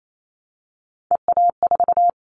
Morse code, heard as a keyed tone, about 28 wpm, 700 Hz. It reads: EA4
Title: Callsign Prefixes for Top 25 DXCC - Repeated 3x - 28wpm